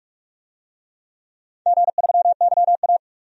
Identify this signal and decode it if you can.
Morse code, heard as a keyed tone, about 46 wpm, 700 Hz. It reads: G3YA
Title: DXCC-100 Call Signs - 4 Characters - 45wpm